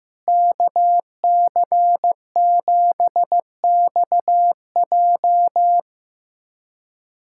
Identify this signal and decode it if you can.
Morse code, heard as a keyed tone, about 15 wpm, 700 Hz. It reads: KC7XJ
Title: USA Callsign DE Callsign - 5 Characters - 15wpm